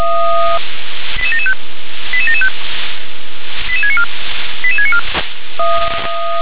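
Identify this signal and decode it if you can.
Five-tone selective call (ZVEI), heard as quick runs of five tones, tones of about 70 ms. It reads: 88775 88775 88694 88694
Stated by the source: ZVEI five-tone selective call received on SDR.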